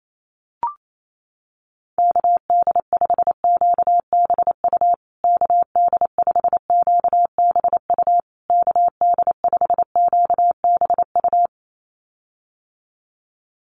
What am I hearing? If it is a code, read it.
Morse code, heard as a keyed tone, about 28 wpm, 700 Hz. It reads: KD5QBU KD5QBU KD5QBU
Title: USA Call Signs Repeated 3x 28wpm